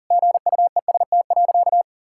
Morse code, heard as a keyed tone, about 37 wpm, 700 Hz. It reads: GUEST.
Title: Sentences from Top 1600 Words 40wpm